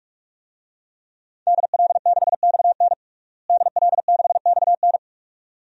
Morse code, heard as a keyed tone, about 46 wpm, 700 Hz. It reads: DL6XN DL6XN
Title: DXCC-100 Call Signs - 5 Characters - Repeated 2x - 45wpm